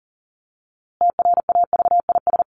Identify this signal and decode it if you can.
Morse code, heard as a keyed tone, about 40 wpm, 700 Hz. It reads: TRAVIS
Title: A Sound Of Thunder 40wpm